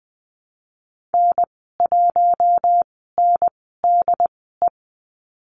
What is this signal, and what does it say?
Morse code, heard as a keyed tone, about 20 wpm, spaced as for 14 wpm, 700 Hz. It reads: N1NDE
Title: 069 Lesson 12g - Letter d - Call Signs 20wpm